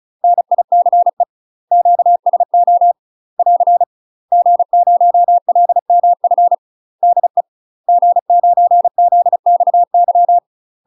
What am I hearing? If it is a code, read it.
Morse code, heard as a keyed tone, about 36 wpm, 700 Hz. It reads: NICE QSO <AR> G0LMF DE G9ZXY
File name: UK MASTER QSOs TRAINING / 35WPM / UK MASTER QSOs 0012